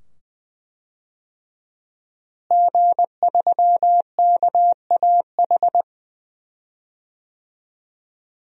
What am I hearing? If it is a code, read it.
Morse code, heard as a keyed tone, about 21 wpm, 700 Hz. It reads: G3KAH
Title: DXCC-100 Call Signs - 5 Characters - Repeated 2x - 20wpm